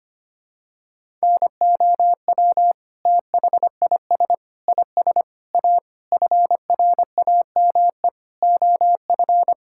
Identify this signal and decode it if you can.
Morse code, heard as a keyed tone, about 25 wpm, 700 Hz. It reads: NOW THIS IS A FRAME OF
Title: Sentences from Top 1000 Words - Review Part 6 - 25wpm